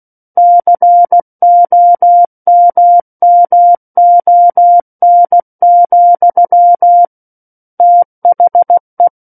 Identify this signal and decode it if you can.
Morse code, heard as a keyed tone, about 16 wpm, 700 Hz. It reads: COMMON, THE
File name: Book One / 07-Chapter 7 - 16 WPM